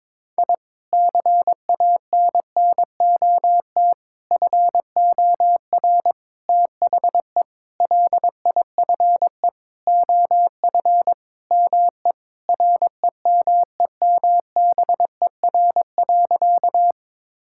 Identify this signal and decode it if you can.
Morse code, heard as a keyed tone, about 22 wpm, 700 Hz. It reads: I CANNOT FOR THE LIFE OF ME REMEMBER.
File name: practice / 22wpm / Sentences with 8-Word Phrases - 22wpm